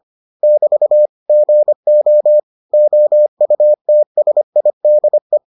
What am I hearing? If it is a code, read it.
Morse code, heard as a keyed tone, about 25 wpm, 600 Hz. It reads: <BT> GO OUTSIDE